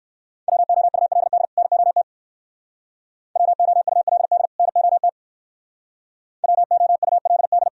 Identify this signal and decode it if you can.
Morse code, heard as a keyed tone, about 58 wpm, 690 Hz. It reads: WOULD NOT WOULD NOT WOULD